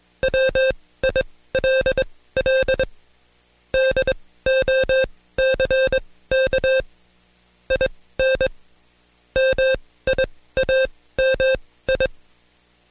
Morse code, heard as a keyed tone, about 21 wpm, spaced as for 16 wpm, 530 Hz. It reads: WILL DOCK IN MIAMI
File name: Code Quick / AudioDownld2 / Seg134-Stepper7 15WPM